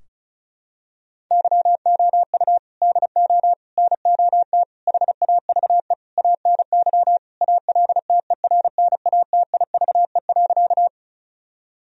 Morse code, heard as a keyed tone, about 34 wpm, 700 Hz. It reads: YOU DO NOT HAVE ANY ALTERNATIVE.
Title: Sentences from Top 1000 Words - Review Part 4 - 35wpm